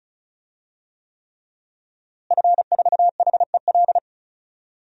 Morse code, heard as a keyed tone, about 34 wpm, 700 Hz. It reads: F4HEL
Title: DXCC-100 Call Signs - 5 Characters - Repeated 3x - 35wpm